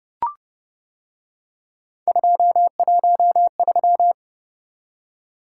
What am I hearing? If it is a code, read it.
Morse code, heard as a keyed tone, about 30 wpm, 700 Hz. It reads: 213